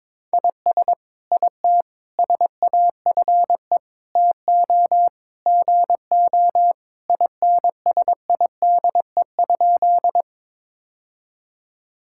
Morse code, heard as a keyed tone, about 22 wpm, 700 Hz. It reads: IS IT SAFE TO GO INSIDE?
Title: Sentences from Top 1000 Words - Review Part 6 - 22wpm